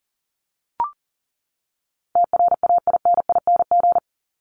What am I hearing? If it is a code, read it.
Morse code, heard as a keyed tone, about 40 wpm, 700 Hz. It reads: TRAINING
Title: Top 501-750 Words 40wpm